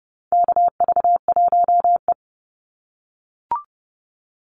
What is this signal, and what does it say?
Morse code, heard as a keyed tone, about 30 wpm, 700 Hz. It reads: KV1E
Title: Call Signs - 30wpm